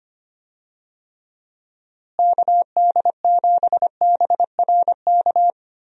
Morse code, heard as a keyed tone, about 25 wpm, 700 Hz. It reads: KD7BRK
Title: USA Call Signs - 6 Characters - Repeated 3x - 25wpm